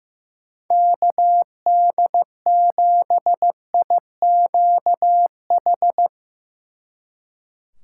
Morse code, heard as a keyed tone, about 16 wpm, 700 Hz. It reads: KD7IQH
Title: Speed Racing - USA 2x3 Call Signs 15wpm